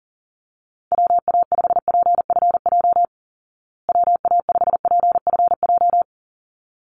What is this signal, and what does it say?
Morse code, heard as a keyed tone, about 40 wpm, 700 Hz. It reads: WA5PFJ WA5PFJ
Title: Call Signs Repeated 3x - 40wpm